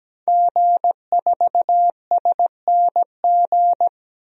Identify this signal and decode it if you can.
Morse code, heard as a keyed tone, about 17 wpm, 700 Hz. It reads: G4SNG
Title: DXCC-100 Call Signs - 5 Characters - 17wpm